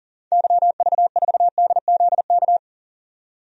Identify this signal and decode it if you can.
Morse code, heard as a keyed tone, about 40 wpm, 700 Hz. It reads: YV4DZK
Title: DXCC-100 Call Signs - 6 Characters - 40wpm